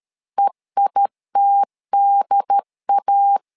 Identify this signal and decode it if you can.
Morse code, heard as a keyed tone, about 13 wpm, 800 Hz. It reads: EITDA
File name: LEZIONI / 3 MWRO / 3 M 12wpm 1541